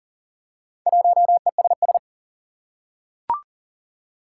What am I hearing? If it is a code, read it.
Morse code, heard as a keyed tone, about 40 wpm, 700 Hz. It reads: 1ESS
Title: Four Challenging Letter-Number Combinations 40wpm